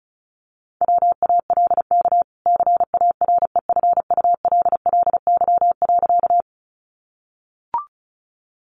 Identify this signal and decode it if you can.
Morse code, heard as a keyed tone, about 34 wpm, 700 Hz. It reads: WALK CAREFULLY.
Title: Sentences from Top 1300 Words 35wpm